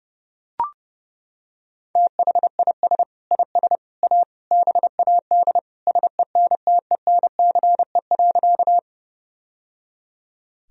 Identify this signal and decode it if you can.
Morse code, heard as a keyed tone, about 30 wpm, 700 Hz. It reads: THIS IS A BAD SENTENCE.
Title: Sentences from Top 1000 Words - Review Part 1 - 30wpm